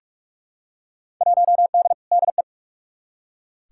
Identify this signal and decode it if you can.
Morse code, heard as a keyed tone, about 46 wpm, 700 Hz. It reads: 1D DE
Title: Field Day Exchanges - 45wpm